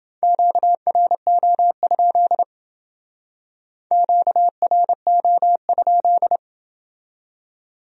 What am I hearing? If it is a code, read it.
Morse code, heard as a keyed tone, about 28 wpm, 700 Hz. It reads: QRO? QRO?